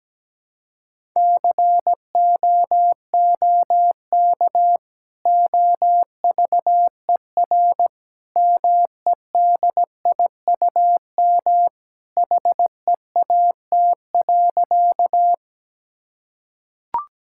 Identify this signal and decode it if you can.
Morse code, heard as a keyed tone, about 17 wpm, 700 Hz. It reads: COOK OVER MEDIUM HEAT.
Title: Sentences from Top 1000 Words - Review Part 6 - 17wpm